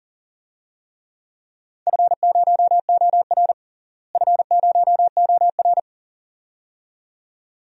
Morse code, heard as a keyed tone, about 40 wpm, 700 Hz. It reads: F0OR F0OR
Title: DXCC-100 Call Signs - 4 Characters - Repeated 2x - 40wpm